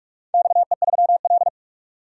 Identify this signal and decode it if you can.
Morse code, heard as a keyed tone, about 41 wpm, 700 Hz. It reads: XE3L
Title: DXCC-100 Call Signs - 4 Characters - 45wpm